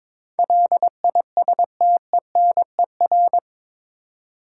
Morse code, heard as a keyed tone, about 22 wpm, 700 Hz. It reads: LISTENER
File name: practice / 22wpm / Eight Letter Words - 22wpm